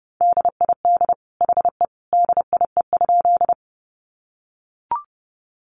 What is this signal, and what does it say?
Morse code, heard as a keyed tone, about 29 wpm, 700 Hz. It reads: DID HE DIE?
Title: Sentences from Top 1300 Words 30wpm